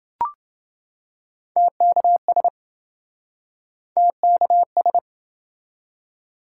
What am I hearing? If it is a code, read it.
Morse code, heard as a keyed tone, about 26 wpm, 700 Hz. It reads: TKS TKS